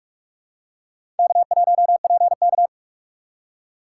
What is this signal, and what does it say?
Morse code, heard as a keyed tone, about 41 wpm, 700 Hz. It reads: K1PK
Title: USA Call Signs - 4 Characters - 4x Character Spacing - 45wpm